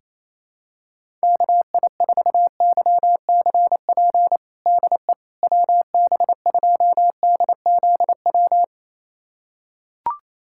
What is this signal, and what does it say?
Morse code, heard as a keyed tone, about 28 wpm, 700 Hz. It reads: KI4YCP DE WB2DZW
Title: USA Callsign DE Callsign - 6 Characters - 28wpm